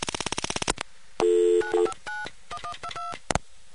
DTMF, heard as telephone touch tones, tones of about 90 ms. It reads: 9591122